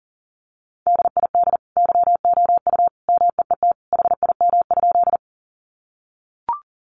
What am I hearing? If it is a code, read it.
Morse code, heard as a keyed tone, about 40 wpm, 700 Hz. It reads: DID YOU MEET HIM?
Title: Sentences from Top 1100 Words 40wpm